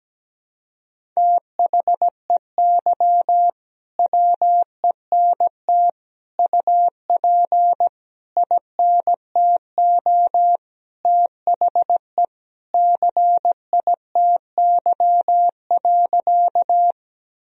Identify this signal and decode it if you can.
Morse code, heard as a keyed tone, about 17 wpm, 700 Hz. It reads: THEY WENT UP INTO THE CITY.
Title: Sentences from Top 1000 Words - Review Part 1 - 17wpm